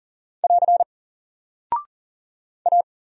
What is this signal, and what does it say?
Morse code, heard as a keyed tone, about 40 wpm, 700 Hz. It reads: <AR> A
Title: Single Character (Letter:Number:Punctuation:Prosign) 40wpm